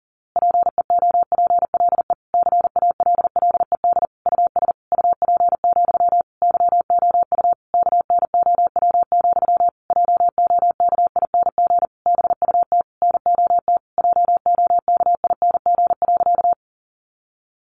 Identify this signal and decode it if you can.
Morse code, heard as a keyed tone, about 40 wpm, 700 Hz. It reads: PEOPLE CALLED US UP, YOU KNOW, JOKING BUT NOT JOKING.
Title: A Sound Of Thunder 40wpm